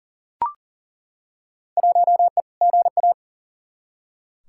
Morse code, heard as a keyed tone, about 40 wpm, 700 Hz. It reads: 1E GA
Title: Field Day Exchanges - 40wpm